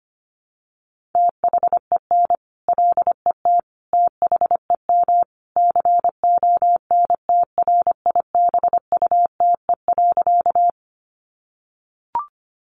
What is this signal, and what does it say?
Morse code, heard as a keyed tone, about 25 wpm, 700 Hz. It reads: THEN LET THEM CONTRIBUTE.
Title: Sentences from Top 1400 Words 25wpm